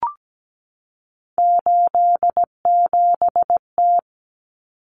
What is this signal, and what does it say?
Morse code, heard as a keyed tone, about 17 wpm, 700 Hz. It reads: 87T
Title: Common Cut Numbers - 3-Digit 17wpm